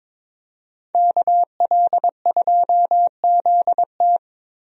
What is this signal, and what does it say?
Morse code, heard as a keyed tone, about 22 wpm, 700 Hz. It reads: KL2ZT
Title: USA Call Signs - 5 Characters - Repeated 2x - 22wpm